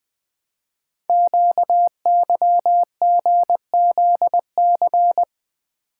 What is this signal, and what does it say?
Morse code, heard as a keyed tone, about 20 wpm, 700 Hz. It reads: QYGZC
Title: Code Groups - 5 Letters - 20wpm